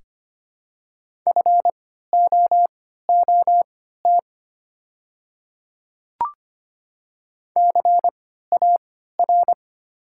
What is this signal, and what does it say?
Morse code, heard as a keyed tone, about 25 wpm, spaced as for 14 wpm, 690 Hz. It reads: FOOT CAR
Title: 3x Character Spacing - Top 500 Words - 25wpm